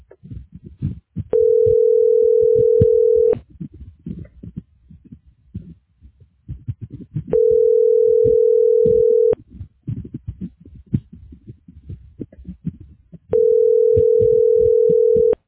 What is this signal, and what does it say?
Ringback tone (North America), 440 + 480 Hz.